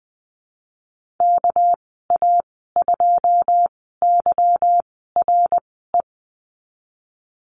Morse code, heard as a keyed tone, about 20 wpm, spaced as for 15 wpm, 700 Hz. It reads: KA2YRE
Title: 200 Lesson 32g - Letter k - Call Signs 20wpm